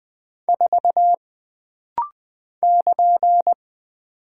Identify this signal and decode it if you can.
Morse code, heard as a keyed tone, about 20 wpm, 700 Hz. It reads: 4 <KN>
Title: Single Character (Letter:Number:Punctuation:Prosign) - Rapid-Fire - 20wpm